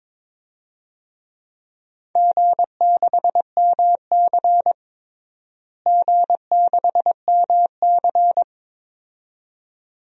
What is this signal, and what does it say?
Morse code, heard as a keyed tone, about 22 wpm, 700 Hz. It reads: G6MC G6MC
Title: DXCC-100 Call Signs - 4 Characters - Repeated 2x - 22wpm